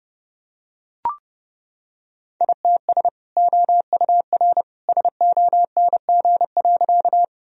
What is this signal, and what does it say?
Morse code, heard as a keyed tone, about 30 wpm, 700 Hz. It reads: ITS OUR SONG.
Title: Sentences from Top 1000 Words - Review Part 2 - 30wpm